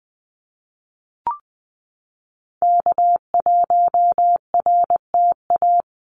Morse code, heard as a keyed tone, about 19 wpm, 700 Hz. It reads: K1RTA
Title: USA Call Signs Repeated 2x 20wpm